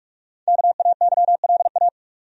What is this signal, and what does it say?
Morse code, heard as a keyed tone, about 46 wpm, 700 Hz. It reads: KAYLA
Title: USA Common Names - 45wpm